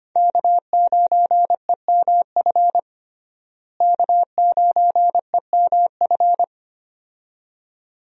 Morse code, heard as a keyed tone, about 25 wpm, 700 Hz. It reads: K9EMF K9EMF
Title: USA Call Signs - 5 Characters - Repeated 2x - 25wpm